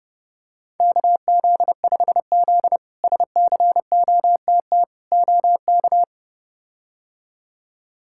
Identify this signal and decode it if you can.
Morse code, heard as a keyed tone, about 32 wpm, 700 Hz. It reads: KZ5Z SCOTT OK
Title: K1USN SST QSO Practice - 30wpm